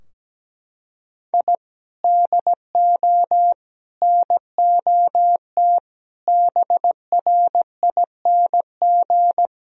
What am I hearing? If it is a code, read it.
Morse code, heard as a keyed tone, about 17 wpm, 700 Hz. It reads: I DO NOT BRING
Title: Sentences from Top 1000 Words - Review Part 3 - 17wpm